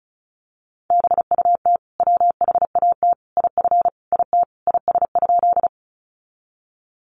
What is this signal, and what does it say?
Morse code, heard as a keyed tone, about 34 wpm, 700 Hz. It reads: BUT WHAT IF IT IS?
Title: Sentences from Top 100 Words 35wpm